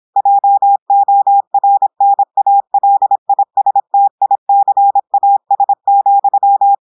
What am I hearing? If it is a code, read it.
Morse code, heard as a keyed tone, about 26 wpm, 800 Hz. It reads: JORNALISTICAS,
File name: portugais texte en clair 2 vitesse 26 mots minutes